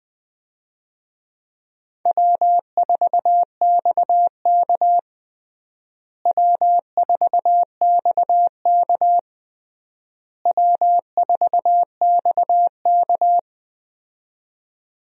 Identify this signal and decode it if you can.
Morse code, heard as a keyed tone, about 20 wpm, 700 Hz. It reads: W4XK W4XK W4XK
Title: USA Call Signs - 4 Characters - Repeated 3x - 20wpm